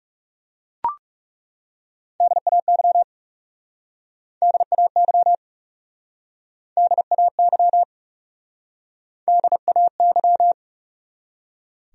Morse code, heard as a keyed tone, about 40 wpm, 700 Hz. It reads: DAY DAY DAY DAY